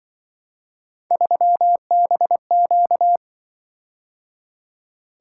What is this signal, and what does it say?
Morse code, heard as a keyed tone, about 24 wpm, 700 Hz. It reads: 3BQ
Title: Speed-Racing - Three Letter-Number Combinations - 20wpm